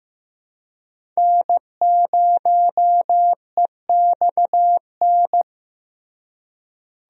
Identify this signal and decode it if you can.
Morse code, heard as a keyed tone, about 16 wpm, 700 Hz. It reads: N0EXN
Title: USA Call Signs - 5 Characters - Repeated 2x - 15wpm